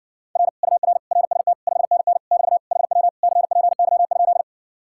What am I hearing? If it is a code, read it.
Morse code, heard as a keyed tone, about 60 wpm, 700 Hz. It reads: R UR RST 5NN <BT> HW CPY?